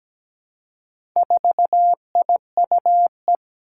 Morse code, heard as a keyed tone, about 17 wpm, 700 Hz. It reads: 4IUE
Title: Four Challenging Letter-Number Combinations 17wpm